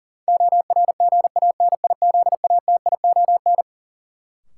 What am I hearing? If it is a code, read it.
Morse code, heard as a keyed tone, about 40 wpm, 700 Hz. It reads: ORGANIZATION
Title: Sentences from Top 1100 Words - New Word Drill Alt - 40wpm